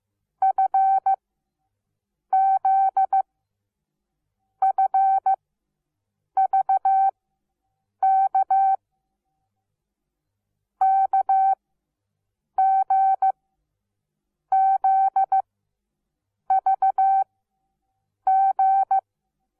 Morse code, heard as a keyed tone, about 15 wpm, spaced as for 6 wpm, 770 Hz. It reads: FZ FVK KGZVG
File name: Onlinelehrgang / morsen / DJ4UF Morsen17